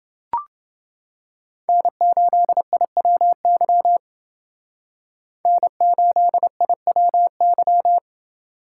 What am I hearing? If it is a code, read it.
Morse code, heard as a keyed tone, about 28 wpm, 700 Hz. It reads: N8IWY N8IWY